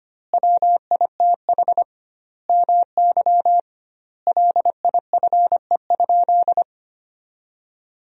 Morse code, heard as a keyed tone, about 25 wpm, 700 Hz. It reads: WITH MY LIFE?
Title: 2x Word-Spacing - Sentences from Top 300 Words - 25wpm